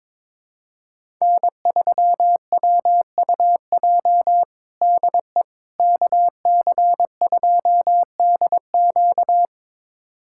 Morse code, heard as a keyed tone, about 21 wpm, 700 Hz. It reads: N3WUJ DE KC2DQ
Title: USA Callsign DE Callsign - 5 Characters - 22wpm